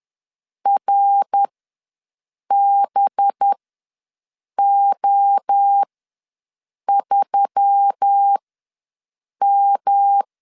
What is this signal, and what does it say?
Morse code, heard as a keyed tone, about 10 wpm, spaced as for 6 wpm, 800 Hz. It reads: RBO3M
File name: LEZIONI / 5 LBVC / 5 LB ws 10wpm 3616